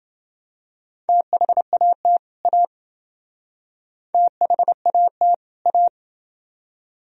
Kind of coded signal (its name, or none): Morse code